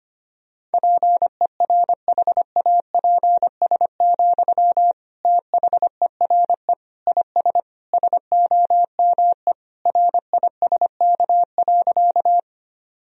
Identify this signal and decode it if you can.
Morse code, heard as a keyed tone, about 25 wpm, 700 Hz. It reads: PERHAPS, THERE IS SOME RISK.
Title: Sentences from Top 1000 Words - Review Part 4 - 25wpm